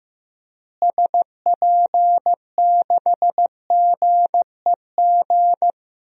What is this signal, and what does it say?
Morse code, heard as a keyed tone, about 15 wpm, 700 Hz. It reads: SP6GEG
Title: DXCC-100 Call Signs - 6 Characters - Repeated 3x - 15wpm